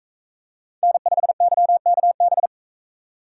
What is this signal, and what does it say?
Morse code, heard as a keyed tone, about 43 wpm, 700 Hz. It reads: N5YKB